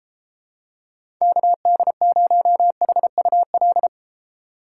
Morse code, heard as a keyed tone, about 33 wpm, 700 Hz. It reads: KD0HUL